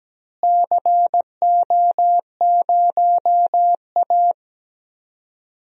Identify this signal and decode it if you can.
Morse code, heard as a keyed tone, about 17 wpm, 700 Hz. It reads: CO0A
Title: DXCC-100 Call Signs - 4 Characters - Repeated 2x - 17wpm